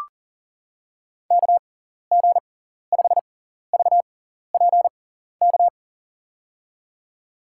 Morse code, heard as a keyed tone, about 40 wpm, spaced as for 14 wpm, 700 Hz. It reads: KG5VPK
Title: USA Call Signs - 6x Character Spacing - 40wpm